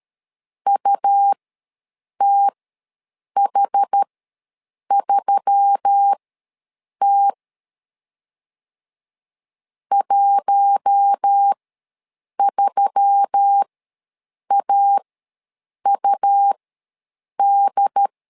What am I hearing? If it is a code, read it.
Morse code, heard as a keyed tone, about 13 wpm, spaced as for 7 wpm, 800 Hz. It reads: UTH3T 13AUD
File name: LEZIONI / 4 Q1S3 ws 12wpm 2038